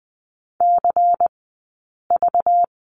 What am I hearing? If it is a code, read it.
Morse code, heard as a keyed tone, about 20 wpm, 700 Hz. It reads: C V
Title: 217 Lesson 35d - Number 0 - Two Characters 20wpm